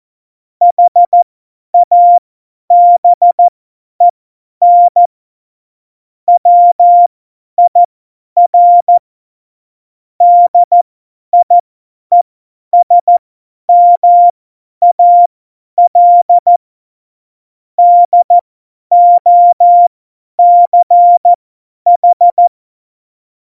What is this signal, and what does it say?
Morse code, heard as a keyed tone, about 14 wpm, spaced as for 10 wpm, 700 Hz. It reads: HABEN WIR DIESMAL DOCH
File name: Morsekurs / Witze / Witz 18. Dezember, 50 F70